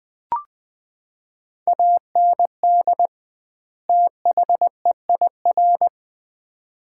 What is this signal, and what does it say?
Morse code, heard as a keyed tone, about 20 wpm, 700 Hz. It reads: AND THEIR
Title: 2x Word-Spacing - Sets of 2 Words - 20wpm